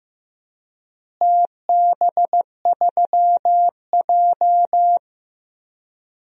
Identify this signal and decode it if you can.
Morse code, heard as a keyed tone, about 15 wpm, 700 Hz. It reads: TB3J
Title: DXCC-100 Call Signs - 4 Characters - Repeated 3x - 15wpm